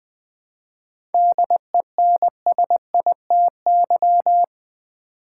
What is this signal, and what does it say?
Morse code, heard as a keyed tone, about 20 wpm, 690 Hz. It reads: DENSITY
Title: 7 Letter Words - First 3 Letters Given - 20wpm